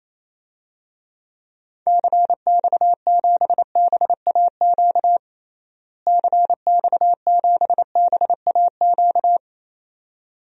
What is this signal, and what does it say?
Morse code, heard as a keyed tone, about 27 wpm, 700 Hz. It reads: CX7BAQ CX7BAQ
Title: DXCC-100 Call Signs - 6 Characters - Repeated 2x - 28wpm